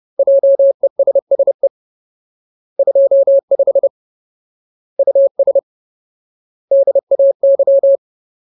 Morse code, heard as a keyed tone, about 30 wpm, 550 Hz. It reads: JESSE 25 US DAY
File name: Ultimate Word List / 30WPM Ultimate